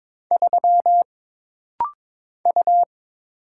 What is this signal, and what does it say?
Morse code, heard as a keyed tone, about 22 wpm, 700 Hz. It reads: 3 U
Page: This rapid-fire timing is geared to push your ICR (Instant Character Recognition) skills.